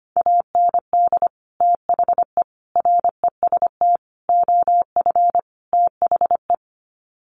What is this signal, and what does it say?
Morse code, heard as a keyed tone, about 25 wpm, 700 Hz. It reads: AND THE REST OF THE
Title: Sets of 5 Words 25wpm